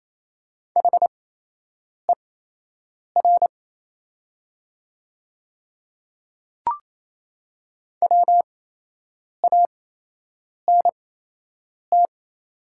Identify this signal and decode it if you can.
Morse code, heard as a keyed tone, about 28 wpm, spaced as for 8 wpm, 700 Hz. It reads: HER WANT
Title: 8x Character Spacing - Top 200 Words - 28wpm